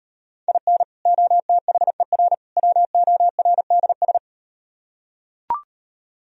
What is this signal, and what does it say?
Morse code, heard as a keyed tone, about 38 wpm, 700 Hz. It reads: IN OTHER WORDS